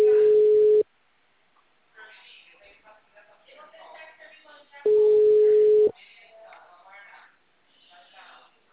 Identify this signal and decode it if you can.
Ringback tone (Europe, 425 Hz), with speech.